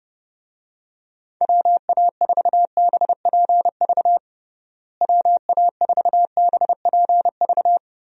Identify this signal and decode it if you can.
Morse code, heard as a keyed tone, about 30 wpm, 700 Hz. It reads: WA4BPV WA4BPV
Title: USA Call Signs - 6 Characters - Repeated 3x - 30wpm